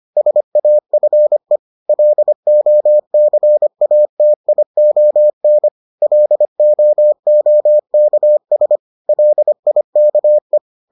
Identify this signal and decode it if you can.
Morse code, heard as a keyed tone, about 25 wpm, 600 Hz. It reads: SAFE LOCATION LOOKS LIKE